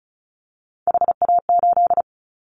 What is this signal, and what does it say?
Morse code, heard as a keyed tone, about 36 wpm, 690 Hz. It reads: HA8
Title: Callsign Prefixes for Top 50 DXCC - 35wpm